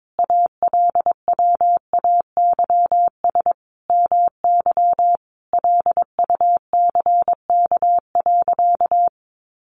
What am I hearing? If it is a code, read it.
Morse code, heard as a keyed tone, about 22 wpm, 700 Hz. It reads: ALWAYS MY LUCK.
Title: Sentences from Top 1500 Words 22wpm